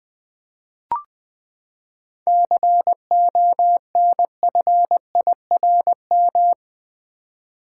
Morse code, heard as a keyed tone, about 20 wpm, 700 Hz. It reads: CONFIRM
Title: Seven Letter Words - 20wpm